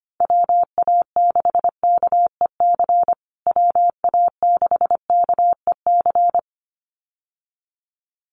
Morse code, heard as a keyed tone, about 25 wpm, 700 Hz. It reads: WA6KEC WA6KEC
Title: USA Call Signs Repeated 2x 25wpm